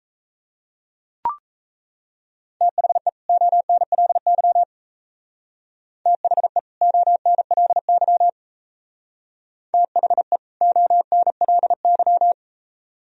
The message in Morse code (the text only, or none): THE ONLY THE ONLY THE ONLY